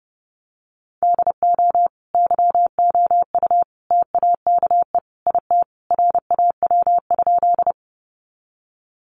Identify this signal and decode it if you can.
Morse code, heard as a keyed tone, about 30 wpm, 700 Hz. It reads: DO YOU TAKE IT RAW?
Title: Sentences from Top 1500 Words 30wpm